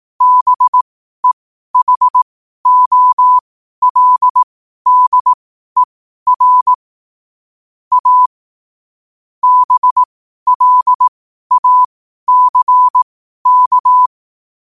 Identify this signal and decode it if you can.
Morse code, heard as a keyed tone, about 18 wpm, spaced as for 12 wpm, 1000 Hz. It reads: BEHOLDER A BLACK
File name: qotdfiles / 12wpm